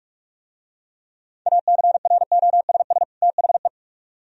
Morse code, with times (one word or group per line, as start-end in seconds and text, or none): 1.5-3.0 ACROSS
3.2-3.7 THE